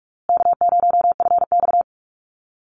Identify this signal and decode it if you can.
Morse code, heard as a keyed tone, about 44 wpm, 700 Hz. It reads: K0FX
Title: Easy Call Signs Repeated 3x - 45wpm